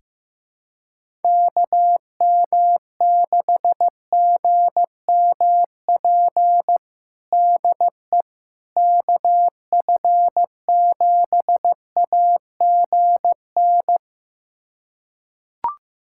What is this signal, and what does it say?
Morse code, heard as a keyed tone, about 16 wpm, 700 Hz. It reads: KM6GMP DE KF7AGN
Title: USA Callsign DE Callsign - 6 Characters - 15wpm